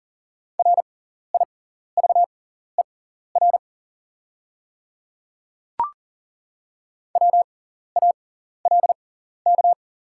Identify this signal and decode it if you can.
Morse code, heard as a keyed tone, about 40 wpm, spaced as for 14 wpm, 690 Hz. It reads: RIVER WALK
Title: 6x Character Spacing - Top 300 Words - 40wpm